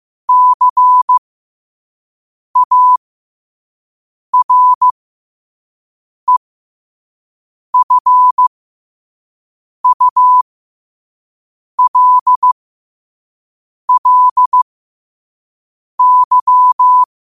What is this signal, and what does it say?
Morse code, heard as a keyed tone, about 15 wpm, spaced as for 5 wpm, 1000 Hz. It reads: CAREFULLY
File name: qotdfiles / 5wpm